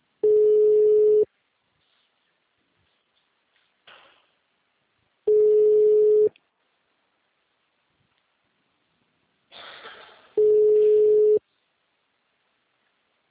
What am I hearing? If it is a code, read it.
Ringback tone (Europe, 425 Hz), with speech.